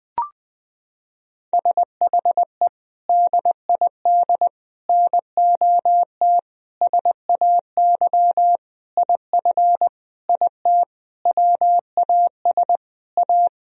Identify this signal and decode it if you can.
Morse code, heard as a keyed tone, about 20 wpm, 700 Hz. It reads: SHE DID NOT SAY IF IT WAS A
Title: Sentences from Top 1000 Words - Review Part 3 - 20wpm